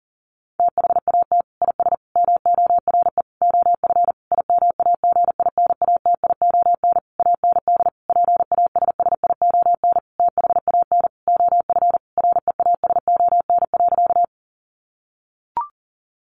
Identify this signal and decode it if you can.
Morse code, heard as a keyed tone, about 40 wpm, 700 Hz. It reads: THAT IS MORE OF IMAGINATION AND PASSION THAN OF REASON.
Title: Sentences from Top 1300 Words 40wpm